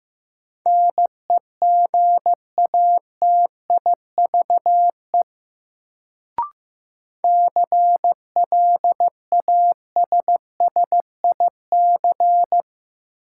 Morse code, heard as a keyed tone, about 15 wpm, 700 Hz. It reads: NEGATIVE CLASSIC
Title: Sentences from Top 1000 Words - New Word Drill - 15wpm